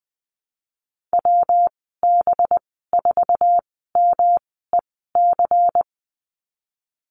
Morse code, heard as a keyed tone, about 20 wpm, spaced as for 15 wpm, 700 Hz. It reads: WB4MEC
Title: 186 Lesson 30g - Letter b - Call Signs 20wpm